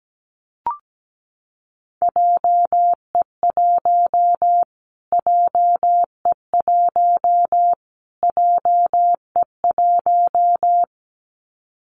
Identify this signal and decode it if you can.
Morse code, heard as a keyed tone, about 17 wpm, 700 Hz. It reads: JE1 JE1 JE1
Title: Callsign Prefixes for Top 25 DXCC - Repeated 3x - 17wpm